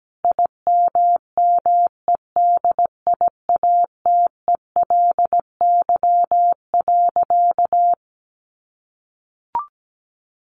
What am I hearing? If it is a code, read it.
Morse code, heard as a keyed tone, about 17 wpm, 700 Hz. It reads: IMMEDIATELY.
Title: Sentences from Top 1100 Words 17wpm